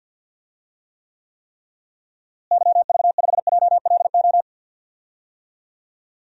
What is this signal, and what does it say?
Morse code, heard as a keyed tone, about 46 wpm, 700 Hz. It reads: YV5JLO
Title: DXCC-100 Call Signs - 6 Characters - Repeated 3x - 50wpm